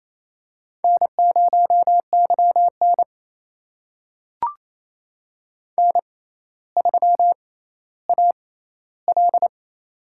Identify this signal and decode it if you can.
Morse code, heard as a keyed tone, about 28 wpm, spaced as for 14 wpm, 700 Hz. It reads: N0YN N3AL
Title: USA Call Signs - 4 Characters - 6x Character Spacing - 28wpm